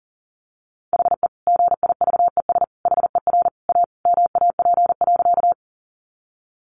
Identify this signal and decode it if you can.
Morse code, heard as a keyed tone, about 40 wpm, 700 Hz. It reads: HE GIVES HER A MAP.
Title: Additional Easy Phrases 40wpm